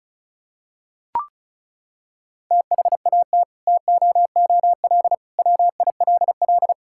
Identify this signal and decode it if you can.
Morse code, heard as a keyed tone, about 36 wpm, 700 Hz. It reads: THAT TOOL WILL
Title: Sentences from Top 1000 Words - Review Part 6 - 35wpm